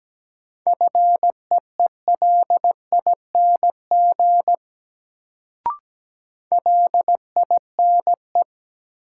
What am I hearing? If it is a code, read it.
Morse code, heard as a keyed tone, about 17 wpm, 700 Hz. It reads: FEELING LINE
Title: Sentences from Top 1200 Words - New Word Drill Alt - 17wpm